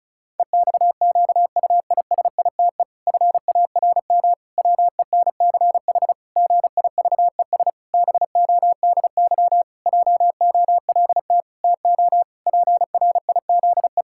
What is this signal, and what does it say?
Morse code, heard as a keyed tone, about 36 wpm, 700 Hz. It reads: EXQUISITE FARM WENCH GIVES BODY JOLT TO PRIZE
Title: Pangram Challenge v2 - 35wpm